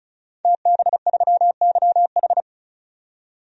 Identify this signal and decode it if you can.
Morse code, heard as a keyed tone, about 34 wpm, 700 Hz. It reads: TB3YH
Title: DXCC-100 Call Signs - 5 Characters - 35wpm